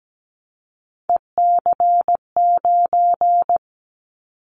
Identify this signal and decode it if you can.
Morse code, heard as a keyed tone, about 17 wpm, 700 Hz. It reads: EC9
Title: Callsign Prefixes for Top 100 DXCC - 17wpm